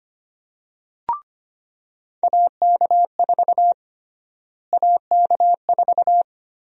Morse code, heard as a keyed tone, about 25 wpm, 700 Hz. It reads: AK4 AK4
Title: USA Call Sign Prefixes - Repeated 2x - 25wpm